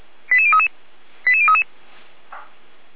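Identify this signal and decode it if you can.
Five-tone selective call (ZVEI), heard as quick runs of five tones, tones of about 70 ms. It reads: 88033 88033
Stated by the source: ZVEI five-tone call received on an SDR, from audio.f85360.feuerwehr-orlamuende.org (Feuerwehr Orlamünde fire service).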